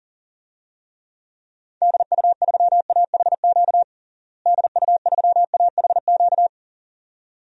Morse code, heard as a keyed tone, about 40 wpm, 700 Hz. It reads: DU3AHQ DU3AHQ
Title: DXCC-100 Call Signs - 6 Characters - Repeated 2x - 40wpm